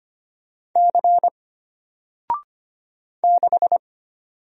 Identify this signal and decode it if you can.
Morse code, heard as a keyed tone, about 25 wpm, 700 Hz. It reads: C 6
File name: Single-Letter-Number-25wpm